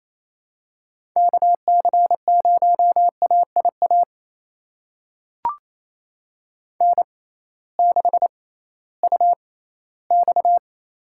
Morse code, heard as a keyed tone, about 28 wpm, spaced as for 16 wpm, 700 Hz. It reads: KC0AIA N6UX
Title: USA Call Signs - 6x Character Spacing - 28wpm